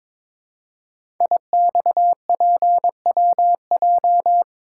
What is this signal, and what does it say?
Morse code, heard as a keyed tone, about 21 wpm, 700 Hz. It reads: IXPWJ
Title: Code Groups - 5 Letters - 22wpm